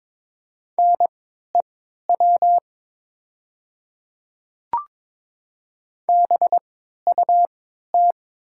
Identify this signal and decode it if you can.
Morse code, heard as a keyed tone, about 22 wpm, spaced as for 12 wpm, 700 Hz. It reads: NEW BUT
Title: 3x Character Spacing - Top 300 Words - 22wpm